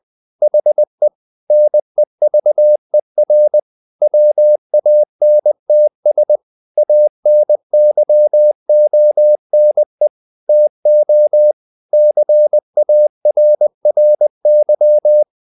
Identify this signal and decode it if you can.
Morse code, heard as a keyed tone, about 20 wpm, 600 Hz. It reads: HE NEVER WANTS ANYONE TO CARRY
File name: the old man and the sea / the old man and the sea part 1 20wpm